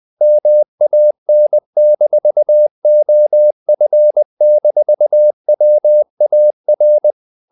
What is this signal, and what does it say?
Morse code, heard as a keyed tone, about 20 wpm, 600 Hz. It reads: MAN-OF-WAR
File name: the old man and the sea / the old man and the sea part 1 20wpm